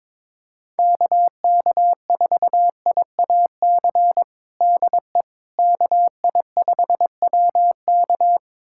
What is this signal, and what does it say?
Morse code, heard as a keyed tone, about 22 wpm, 700 Hz. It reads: KK4IAC DE KI5WK
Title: USA Callsign DE Callsign 22wpm